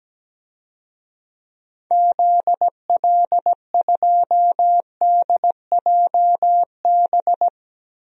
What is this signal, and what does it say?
Morse code, heard as a keyed tone, about 17 wpm, 700 Hz. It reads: ZL2DJB
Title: DXCC-100 Call Signs - 6 Characters - 17wpm